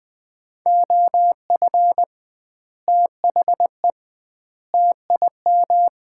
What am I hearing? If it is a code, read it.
Morse code, heard as a keyed tone, about 20 wpm, 700 Hz. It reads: OF THE TIM
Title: 2x Word-Spacing - Sets of 3 Words - 20wpm